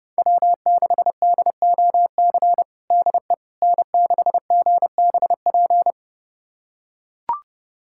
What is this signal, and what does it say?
Morse code, heard as a keyed tone, about 30 wpm, 700 Hz. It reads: W6DOC DE N6GBP
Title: USA Callsign DE Callsign - 5 Characters - 30wpm